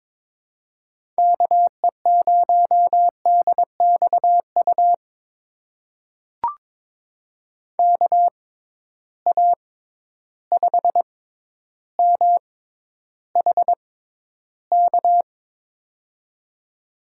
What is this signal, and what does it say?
Morse code, heard as a keyed tone, about 22 wpm, spaced as for 11 wpm, 700 Hz. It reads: KE0DXU KA5MHK
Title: USA Call Signs - 6 Characters - 6x Character Spacing - 22wpm